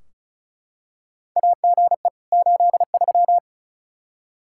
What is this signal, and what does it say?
Morse code, heard as a keyed tone, about 34 wpm, 700 Hz. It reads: AGE 83
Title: Practice QSO - Exchange 3 - More Info - 35wpm